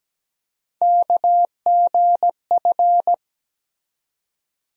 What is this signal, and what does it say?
Morse code, heard as a keyed tone, about 17 wpm, 700 Hz. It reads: KGF
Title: USA Call Sign Postfixes - 3 Letters 17wpm